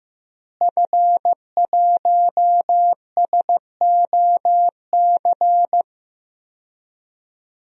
Morse code, heard as a keyed tone, about 16 wpm, 700 Hz. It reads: F1SOC
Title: DXCC-100 Call Signs - 5 Characters - Repeated 2x - 15wpm